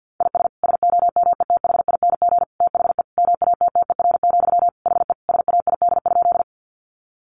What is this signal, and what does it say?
Morse code, heard as a keyed tone, about 50 wpm, 700 Hz. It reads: IS SOMETHING THE MATTER, HE SAID?
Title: Sentences from Top 700 Words 50wpm